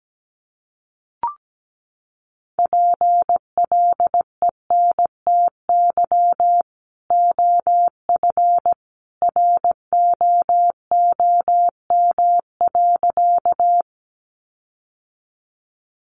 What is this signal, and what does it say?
Morse code, heard as a keyed tone, about 17 wpm, 700 Hz. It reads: PLENTY OF ROOM.
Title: Sentences from Top 1300 Words 17wpm